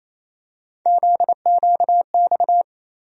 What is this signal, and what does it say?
Morse code, heard as a keyed tone, about 28 wpm, 700 Hz. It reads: ZQX
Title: USA Call Sign Postfixes - 3 Letters 28wpm